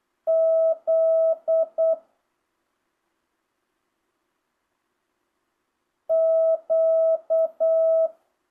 Morse code, heard as a keyed tone, about 8 wpm, 650 Hz. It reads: Z Q
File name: jh8cbh / 2025cw